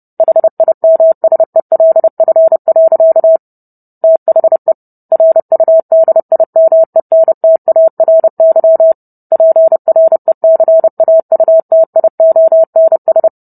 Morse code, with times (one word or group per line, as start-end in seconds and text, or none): 0.2-3.4 HIMSELF.
4.0-4.7 THE
5.1-8.9 RUDIMENTARY
9.3-13.3 PRECAUTIONS